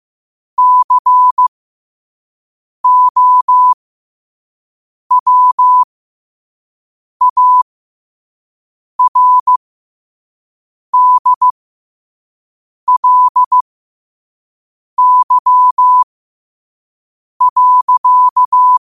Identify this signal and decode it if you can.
Morse code, heard as a keyed tone, about 15 wpm, spaced as for 5 wpm, 1000 Hz. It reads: COWARDLY.
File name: qotdfiles / 5wpm